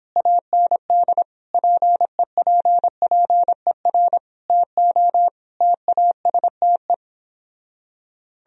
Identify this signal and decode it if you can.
Morse code, heard as a keyed tone, about 29 wpm, 700 Hz. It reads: AND PEPPER TO TASTE